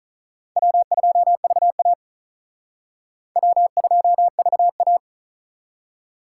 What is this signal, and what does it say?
Morse code, heard as a keyed tone, about 45 wpm, 700 Hz. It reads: W2VA W2VA